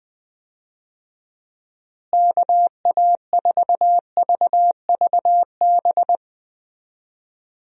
Morse code, heard as a keyed tone, about 21 wpm, 700 Hz. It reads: KA4VVB
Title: USA Call Signs - 6 Characters - Repeated 2x - 20wpm